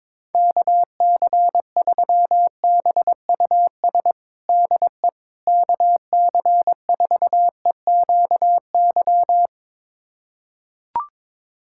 Morse code, heard as a keyed tone, about 22 wpm, 700 Hz. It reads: KC3BUS DE KC4EQY
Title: USA Callsign DE Callsign 22wpm